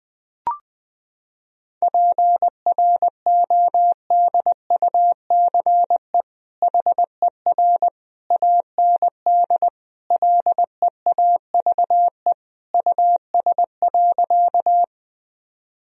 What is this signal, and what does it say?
Morse code, heard as a keyed tone, about 21 wpm, 700 Hz. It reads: PRODUCE HER AND LEAVE US.
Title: Sentences from Top 1000 Words - Review Part 3 - 20wpm